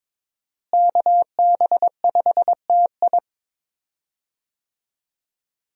Morse code, heard as a keyed tone, about 22 wpm, 700 Hz. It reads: KB5TI
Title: USA Call Signs - 5 Characters - Repeated 2x - 22wpm